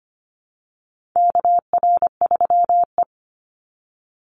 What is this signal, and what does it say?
Morse code, heard as a keyed tone, about 25 wpm, 700 Hz. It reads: KR3E
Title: Call Signs Repeated 2x - 25wpm